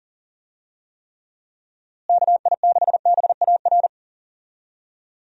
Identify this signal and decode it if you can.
Morse code, heard as a keyed tone, about 40 wpm, 700 Hz. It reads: KI6BAR
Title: USA Call Signs - 6 Characters - Repeated 2x - 40wpm